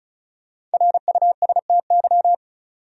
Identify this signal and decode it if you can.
Morse code, heard as a keyed tone, about 34 wpm, 700 Hz. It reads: RUSTY
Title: USA Common Names - 35wpm